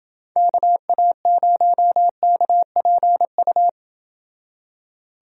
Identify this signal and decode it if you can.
Morse code, heard as a keyed tone, about 29 wpm, 700 Hz. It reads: KA0KPU